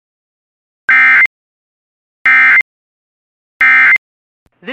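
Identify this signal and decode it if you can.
Emergency Alert System (SAME) tones, heard as harsh buzzing bursts. It is heard from 0.9 to 4.0 s.
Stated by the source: Emergency alert tones from a Weatherscan IntelliStar weather channel system.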